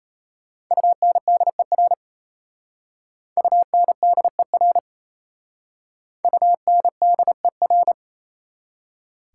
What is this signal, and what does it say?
Morse code, heard as a keyed tone, about 35 wpm, 690 Hz. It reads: UNDER UNDER UNDER
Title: Speed-Racing Prefixes and Suffixes 28wpm